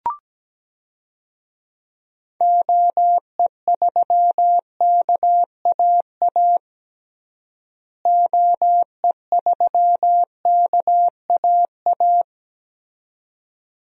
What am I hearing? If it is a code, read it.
Morse code, heard as a keyed tone, about 17 wpm, 700 Hz. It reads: OE3KAA OE3KAA
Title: DXCC-100 Call Signs - 6 Characters - Repeated 2x - 17wpm